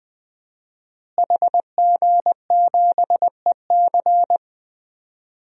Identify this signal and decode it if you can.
Morse code, heard as a keyed tone, about 20 wpm, 700 Hz. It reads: HG7EC
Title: DXCC-100 Call Signs - 5 Characters - Repeated 3x - 20wpm